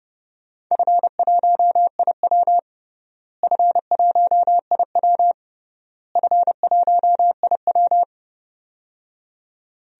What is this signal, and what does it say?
Morse code, heard as a keyed tone, about 31 wpm, 700 Hz. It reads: F1IW F1IW F1IW
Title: DXCC-100 Call Signs - 4 Characters - Repeated 3x - 30wpm